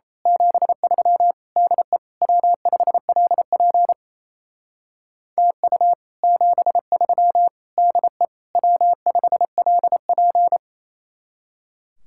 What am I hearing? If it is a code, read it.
Morse code, heard as a keyed tone, about 31 wpm, 700 Hz. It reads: 73 DE W5LP TU 73 DE W5LP
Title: Speed-Racing Common QSO Phrases - 28wpm